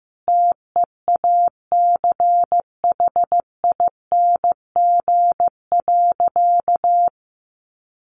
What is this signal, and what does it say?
Morse code, heard as a keyed tone, about 15 wpm, 700 Hz. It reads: TEACHING.
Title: Sentences from Top 1100 Words 15wpm